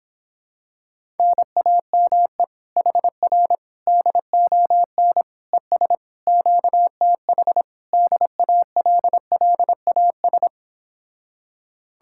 Morse code, heard as a keyed tone, about 26 wpm, 700 Hz. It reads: NAME HR DON ES QTH DALLAS